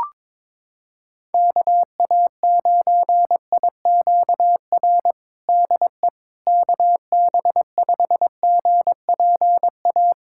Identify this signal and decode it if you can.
Morse code, heard as a keyed tone, about 22 wpm, 700 Hz. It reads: KA9IQR DE KB5GPA
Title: USA Callsign DE Callsign - 6 Characters - 22wpm